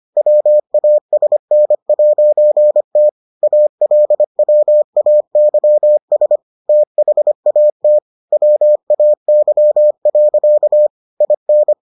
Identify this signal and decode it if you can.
Morse code, heard as a keyed tone, about 25 wpm, 600 Hz. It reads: WASN'T ALWAYS THAT WAY. IN